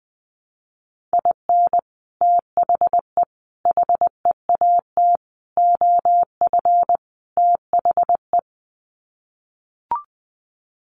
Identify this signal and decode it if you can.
Morse code, heard as a keyed tone, about 20 wpm, 700 Hz. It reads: IN THE HEAT OF THE
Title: Sets of 5 Words Encore2 20wpm